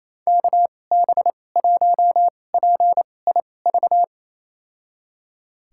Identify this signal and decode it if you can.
Morse code, heard as a keyed tone, about 27 wpm, spaced as for 20 wpm, 700 Hz. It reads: KB1PIV
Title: USA Call Signs - 6 Characters - 2x Character Spacing - 28wpm